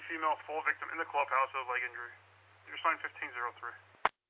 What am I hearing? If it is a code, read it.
Emergency Alert System (SAME) tones, heard as harsh buzzing bursts, with speech. It is heard from 0.6 to 3.3 s.